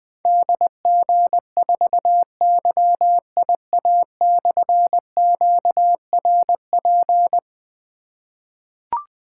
Morse code, heard as a keyed tone, about 20 wpm, 700 Hz. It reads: DG4YIA/QRP
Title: Difficult Call Signs 20wpm